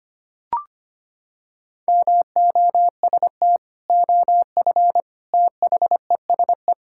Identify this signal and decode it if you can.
Morse code, heard as a keyed tone, about 25 wpm, 700 Hz. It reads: MOST OF THESE
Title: Sentences from Top 1000 Words - Review Part 5 - 25wpm